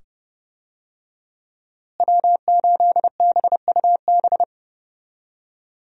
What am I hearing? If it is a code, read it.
Morse code, heard as a keyed tone, about 30 wpm, 700 Hz. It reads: W8BUB
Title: USA Call Signs - 5 Characters - Repeated 2x - 30wpm